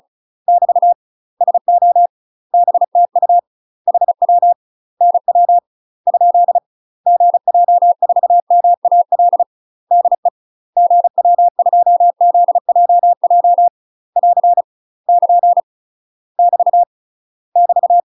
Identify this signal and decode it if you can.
Morse code, heard as a keyed tone, about 34 wpm, 700 Hz. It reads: <BT> SO BTU HW NW ? GJ4MAL DE GW2ZJJ <AR> <KN> <BT> <BT>
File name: UK MASTER QSOs TRAINING / 35WPM / UK MASTER QSOs 0000